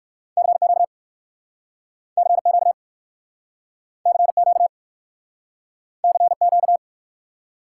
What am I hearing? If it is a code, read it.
Morse code, heard as a keyed tone, about 60 wpm, 700 Hz. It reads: CQ CQ CQ CQ